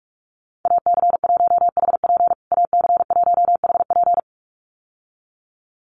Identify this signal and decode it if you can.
Morse code, heard as a keyed tone, about 44 wpm, 700 Hz. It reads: AC1HP AC1HP
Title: USA Call Signs Repeated 2x 45wpm